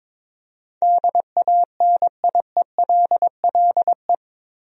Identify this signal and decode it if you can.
Morse code, heard as a keyed tone, about 22 wpm, 700 Hz. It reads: DANIELLE
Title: USA Common Names - 22wpm